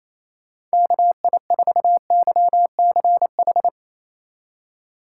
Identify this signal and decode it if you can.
Morse code, heard as a keyed tone, about 36 wpm, 700 Hz. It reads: KI4YCH